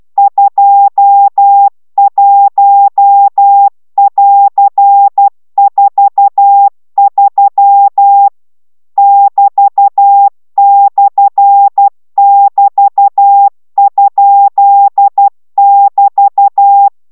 Morse code, heard as a keyed tone, about 12 wpm, 800 Hz. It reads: 21<AR>43 <BT>/<BT>?<BT>